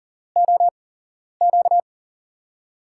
Morse code, heard as a keyed tone, about 40 wpm, 700 Hz. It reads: O Q
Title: USA Call Signs - 5 Characters - 8x Character Spacing - 40wpm